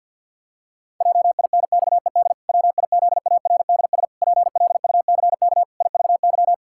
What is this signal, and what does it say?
Morse code, heard as a keyed tone, about 50 wpm, 700 Hz. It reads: JINXED WIZARDS PLUCK IVY
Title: Pangram Challenge v2 - 50wpm